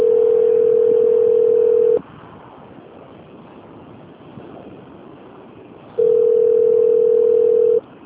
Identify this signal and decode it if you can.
Ringback tone (North America), 440 + 480 Hz.
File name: out- - - - - .2214